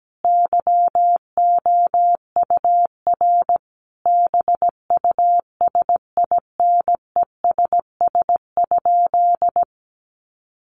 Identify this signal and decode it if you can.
Morse code, heard as a keyed tone, about 17 wpm, 700 Hz. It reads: YOUR BUSINESS?
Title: Sentences from Top 1400 Words 17wpm